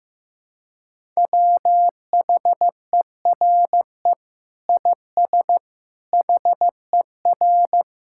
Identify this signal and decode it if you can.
Morse code, heard as a keyed tone, about 15 wpm, 700 Hz. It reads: WHERE IS HER
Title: Sentences from Top 1000 Words - Review Part 5 - 15wpm